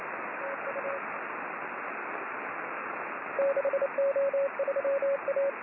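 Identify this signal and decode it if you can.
Morse code, heard as a keyed tone, about 29 wpm, 550 Hz, with radio noise. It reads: TU 6O3A
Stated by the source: Morse signal on the 6 m band.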